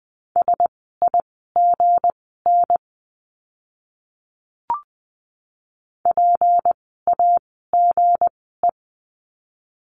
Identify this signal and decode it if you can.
Morse code, heard as a keyed tone, about 20 wpm, spaced as for 15 wpm, 700 Hz. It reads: SIGN PAGE
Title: 160 Lesson 26f - Letter g - Words 20wpm